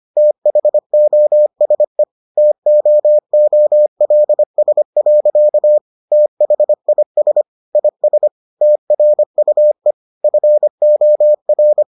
Morse code, heard as a keyed tone, about 25 wpm, 600 Hz. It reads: THOSE TOOLS. THIS IS TRUE FOR